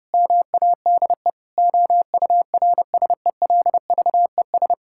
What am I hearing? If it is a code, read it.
Morse code, heard as a keyed tone, about 30 wpm, 700 Hz. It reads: MADE OURSELVES
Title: Sentences from Top 1600 Words 30wpm